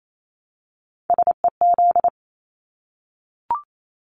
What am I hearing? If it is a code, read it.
Morse code, heard as a keyed tone, about 28 wpm, 700 Hz. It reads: SEZ
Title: Abbreviations 28wpm